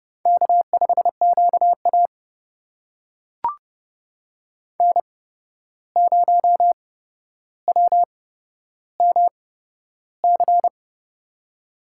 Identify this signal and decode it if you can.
Morse code, heard as a keyed tone, about 30 wpm, spaced as for 12 wpm, 700 Hz. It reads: K5QA N0WMC
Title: USA Call Signs - 8x Character Spacing - 30wpm